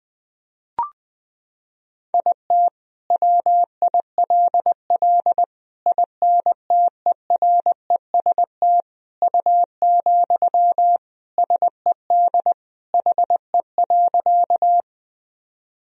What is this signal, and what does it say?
Morse code, heard as a keyed tone, about 20 wpm, 700 Hz. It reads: IT WILL INTEREST U, SED HE.
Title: Sentences from Top 400 Words and Abbreviations 20wpm